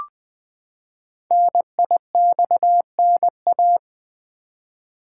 Morse code, heard as a keyed tone, about 21 wpm, 700 Hz. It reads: NIXNA
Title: Code Groups - 5 Letters - 20wpm